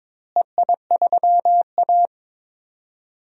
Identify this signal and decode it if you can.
Morse code, heard as a keyed tone, about 22 wpm, 700 Hz. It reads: EI3A
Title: DXCC-100 Call Signs - 4 Characters - Repeated 3x - 22wpm